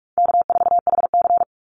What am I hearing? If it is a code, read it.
Morse code, heard as a keyed tone, about 44 wpm, 700 Hz. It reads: K4HC
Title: Easy Call Signs - 45wpm